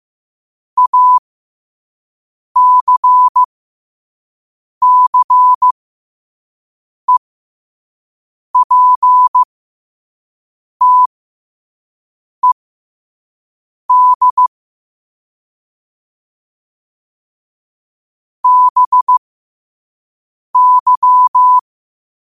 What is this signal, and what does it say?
Morse code, heard as a keyed tone, about 15 wpm, spaced as for 5 wpm, 1000 Hz. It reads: ACCEPTED BY
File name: qotdfiles / 5wpm